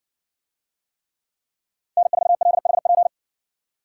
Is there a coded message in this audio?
Yes: Morse code from 2.0 to 3.1 s, heard as a keyed tone, about 60 wpm, 700 Hz. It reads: N4LHP